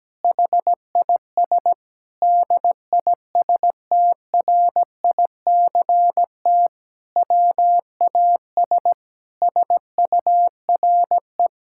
Morse code, heard as a keyed tone, about 17 wpm, 700 Hz. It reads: HIS DISTRICT WAS SURE
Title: Sentences from Top 1600 Words 17wpm